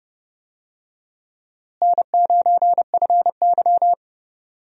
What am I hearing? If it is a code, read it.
Morse code, heard as a keyed tone, about 30 wpm, 700 Hz. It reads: N9FY
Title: USA Call Signs - 4 Characters - Repeated 2x - 30wpm